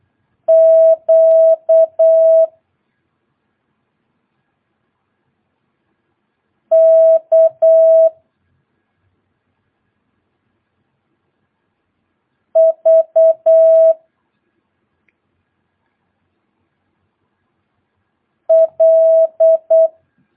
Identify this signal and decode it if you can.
Morse code, heard as a keyed tone, about 8 wpm, 650 Hz. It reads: Q K V L
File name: jh8cbh / 2022cw